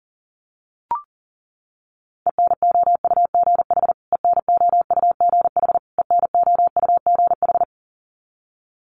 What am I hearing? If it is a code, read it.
Morse code, heard as a keyed tone, about 40 wpm, 700 Hz. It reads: ENOUGH ENOUGH ENOUGH
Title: Top 500 Words Repeated 40wpm